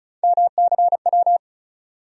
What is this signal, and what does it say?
Morse code, heard as a keyed tone, about 34 wpm, 700 Hz. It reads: MCW
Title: USA Call Sign Postfixes - 3 Letters 35wpm